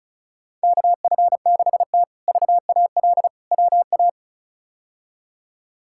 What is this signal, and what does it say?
Morse code, heard as a keyed tone, about 36 wpm, 700 Hz. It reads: KF6T VAL WA
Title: K1USN SST QSO Practice - 35wpm